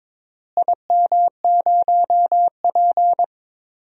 Morse code, heard as a keyed tone, about 22 wpm, 700 Hz. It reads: IM0P
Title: DXCC-100 Call Signs - 4 Characters - Repeated 3x - 22wpm